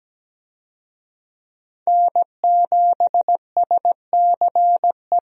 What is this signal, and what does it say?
Morse code, heard as a keyed tone, about 17 wpm, 700 Hz. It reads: N7SCE
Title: USA Call Signs - 5 Characters - Repeated 3x - 17wpm